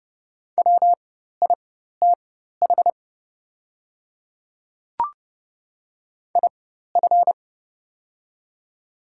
Morse code, heard as a keyed tone, about 31 wpm, spaced as for 14 wpm, 690 Hz. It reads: WITH IF
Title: 4x Character Spacing - Top 300 Words - 30wpm